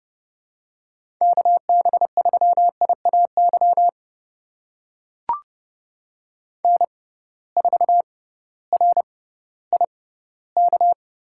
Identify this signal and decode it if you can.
Morse code, heard as a keyed tone, about 30 wpm, spaced as for 16 wpm, 700 Hz. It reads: KB3IAY N4RIK
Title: USA Call Signs - 6x Character Spacing - 30wpm